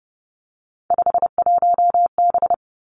Morse code, heard as a keyed tone, about 31 wpm, 700 Hz. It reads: 51B
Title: Three Challenging Letter-Number Combinations 30wpm